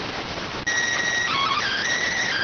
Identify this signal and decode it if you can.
SSTV (slow-scan television), PD 290.